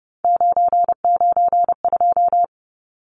Morse code, heard as a keyed tone, about 30 wpm, 700 Hz. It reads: 992
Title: Numbers - 3-Digit 30wpm